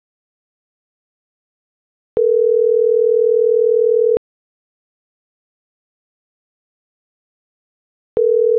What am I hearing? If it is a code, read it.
Ringback tone (North America), 440 + 480 Hz.